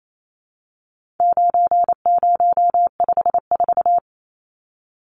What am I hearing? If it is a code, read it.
Morse code, heard as a keyed tone, about 28 wpm, 700 Hz. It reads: 9054
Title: Numbers - 4-Digit 28wpm